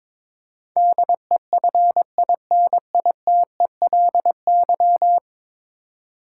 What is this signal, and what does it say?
Morse code, heard as a keyed tone, about 22 wpm, 700 Hz. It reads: DEFINITELY
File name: practice / 22wpm / 10 Letter Words - First 3 Letters Given - 22wpm